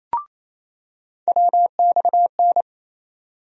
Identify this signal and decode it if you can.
Morse code, heard as a keyed tone, about 28 wpm, 700 Hz. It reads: WXN
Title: USA Call Sign Postfixes - 3 Letters 28wpm